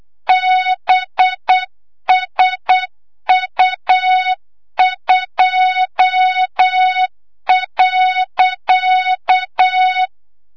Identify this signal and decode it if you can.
Morse code, heard as a keyed tone, about 8 wpm, 3700 Hz, with noise. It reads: BSU2.
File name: Morze-HA2MN / hang / 25 GYAKORLAT